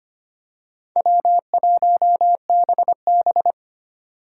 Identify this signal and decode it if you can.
Morse code, heard as a keyed tone, about 25 wpm, 700 Hz. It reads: W1BB
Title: USA Call Signs - 4 Characters - 3x Character Spacing - 25wpm